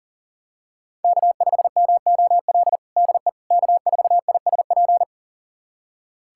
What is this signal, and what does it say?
Morse code, heard as a keyed tone, about 37 wpm, 700 Hz. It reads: K5MOL DE K4ISP
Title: USA Callsign DE Callsign - 5 Characters - 40wpm